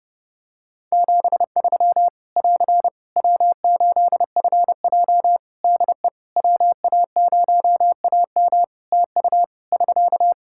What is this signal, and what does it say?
Morse code, heard as a keyed tone, about 30 wpm, 700 Hz. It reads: 73 <AR> W8FJ DE WA0AM TU <SK>
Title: Practice QSO - Exchange 4 - Ending - 30wpm